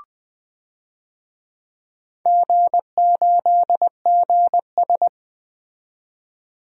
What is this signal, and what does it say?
Morse code, heard as a keyed tone, about 20 wpm, 700 Hz. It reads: G8GS
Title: DXCC-100 Call Signs - 4 Characters - 20wpm